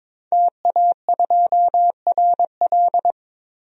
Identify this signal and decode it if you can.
Morse code, heard as a keyed tone, about 22 wpm, 700 Hz. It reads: TA2RL
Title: DXCC-100 Call Signs - 5 Characters - 22wpm